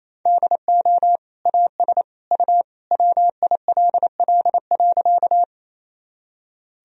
Morse code, heard as a keyed tone, about 28 wpm, 700 Hz. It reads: DO AS U WILL.
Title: Sentences from Top 100 Words and Abbreviations 28wpm